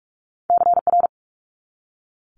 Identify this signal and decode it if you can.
Morse code, heard as a keyed tone, about 46 wpm, 700 Hz. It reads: CR
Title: Multi-letter Phonemes 45wpm